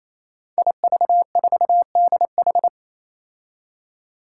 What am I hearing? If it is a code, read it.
Morse code, heard as a keyed tone, about 28 wpm, 700 Hz. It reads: IV4DH
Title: Five Challenging Letter-Number Combinations 28wpm